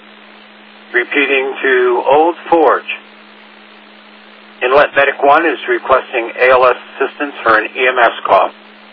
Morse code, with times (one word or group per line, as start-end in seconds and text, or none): none